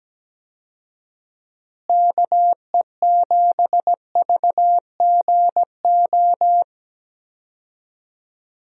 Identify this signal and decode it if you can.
Morse code, heard as a keyed tone, about 17 wpm, 700 Hz. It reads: KE7VGO
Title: USA Call Signs - 6 Characters - Repeated 3x - 17wpm